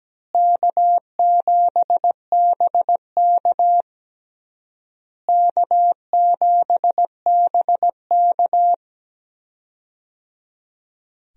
Morse code, heard as a keyed tone, about 17 wpm, 700 Hz. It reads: K7BK K7BK
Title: USA Call Signs - 4 Characters - Repeated 3x - 17wpm